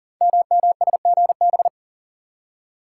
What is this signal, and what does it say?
Morse code, heard as a keyed tone, about 39 wpm, 690 Hz. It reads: MMSGB
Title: Code Groups - 5 Letters - 40wpm